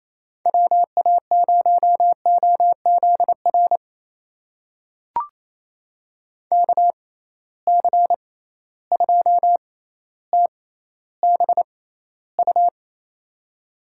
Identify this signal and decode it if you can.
Morse code, heard as a keyed tone, about 28 wpm, spaced as for 14 wpm, 700 Hz. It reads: WA0OZR KC2TBU
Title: USA Call Signs - 6 Characters - 6x Character Spacing - 28wpm